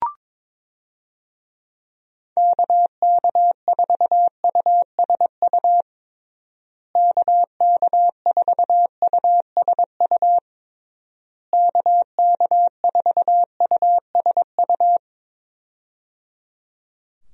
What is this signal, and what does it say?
Morse code, heard as a keyed tone, about 22 wpm, 700 Hz. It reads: KK4USU KK4USU KK4USU
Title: USA Call Signs - 6 Characters - Repeated 3x - 22wpm